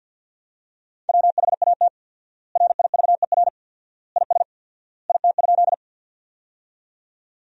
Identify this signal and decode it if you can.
Morse code, heard as a keyed tone, about 49 wpm, 700 Hz. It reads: WHAT RIVER IS IT?
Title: 4x Word-Spacing - Sentences from Top 300 Words - 50wpm